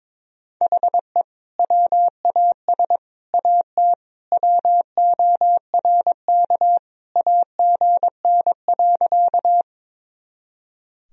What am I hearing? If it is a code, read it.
Morse code, heard as a keyed tone, about 22 wpm, 700 Hz. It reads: HE WAS AT WORK AGN.
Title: Sentences from Top 200 Words and Abbreviations 22wpm